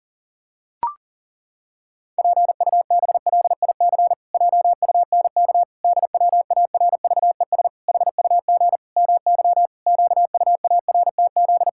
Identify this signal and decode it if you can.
Morse code, heard as a keyed tone, about 40 wpm, 700 Hz. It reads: PUBLIC JUNK DWARVES HUG MY QUARTZ
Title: Pangram Challenge v2 - 40wpm